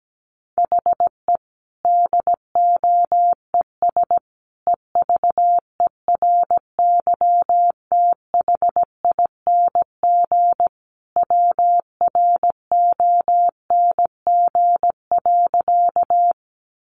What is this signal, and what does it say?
Morse code, heard as a keyed tone, about 17 wpm, 700 Hz. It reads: HE DOES EVERYTHING WRONG.
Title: Sentences from Top 900 Words 17wpm